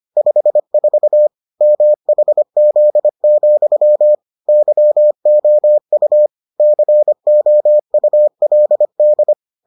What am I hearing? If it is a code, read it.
Morse code, heard as a keyed tone, about 25 wpm, 600 Hz. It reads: 54 MHZ, YOU COULD